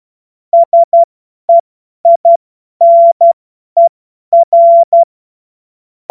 Morse code, heard as a keyed tone, about 12 wpm, spaced as for 10 wpm, 680 Hz. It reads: SEINER
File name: Morsekurs / Witze / Witz 07. Juni 2022 50 F70